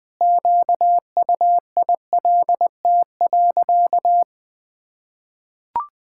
Morse code, heard as a keyed tone, about 20 wpm, 700 Hz. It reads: QUILT.
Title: Pangram Challenge v2 - 20wpm